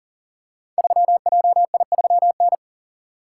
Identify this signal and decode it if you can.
Morse code, heard as a keyed tone, about 38 wpm, 700 Hz. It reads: 3JI3N
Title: Code Groups - 5 Letter:Numbers - 40wpm